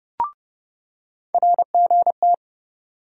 Morse code, heard as a keyed tone, about 29 wpm, 700 Hz. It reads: RGT
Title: USA Call Sign Postfixes - 3 Letters 30wpm